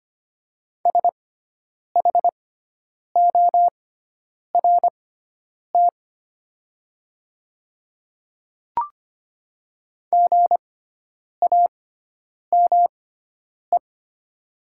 Morse code, heard as a keyed tone, about 25 wpm, spaced as for 9 wpm, 700 Hz. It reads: SHORT GAME
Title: 6x Character Spacing - Top 500 Words - 25wpm